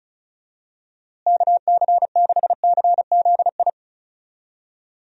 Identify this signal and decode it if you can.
Morse code, heard as a keyed tone, about 36 wpm, 700 Hz. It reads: KC6CZI
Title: USA Call Signs - 6 Characters - Repeated 2x - 35wpm